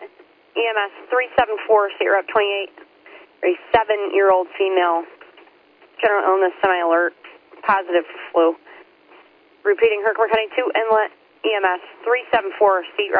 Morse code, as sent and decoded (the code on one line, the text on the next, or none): none